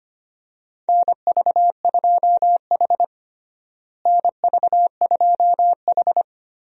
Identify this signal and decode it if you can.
Morse code, heard as a keyed tone, about 25 wpm, 700 Hz. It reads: NV2H NV2H
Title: USA Call Signs - 4 Characters - Repeated 2x - 25wpm